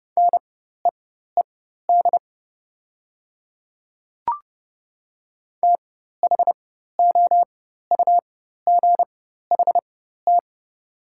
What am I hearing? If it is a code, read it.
Morse code, heard as a keyed tone, about 30 wpm, spaced as for 14 wpm, 700 Hz. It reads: NEED THOUGHT
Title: 4x Character Spacing - Top 300 Words - 30wpm